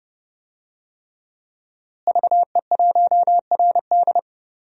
Morse code, heard as a keyed tone, about 30 wpm, 700 Hz. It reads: VE1RD
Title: DXCC-100 Call Signs - 5 Characters - Repeated 3x - 30wpm